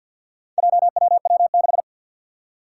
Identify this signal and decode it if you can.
Morse code, heard as a keyed tone, about 51 wpm, 700 Hz. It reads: JWW6
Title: Four Challenging Letter-Number Combinations 50wpm